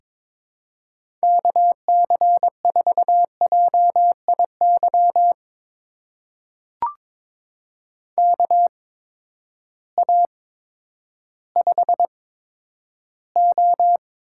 Morse code, heard as a keyed tone, about 22 wpm, spaced as for 11 wpm, 700 Hz. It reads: KC4JIY KA5O
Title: USA Call Signs - 6 Characters - 8x Character Spacing - 22wpm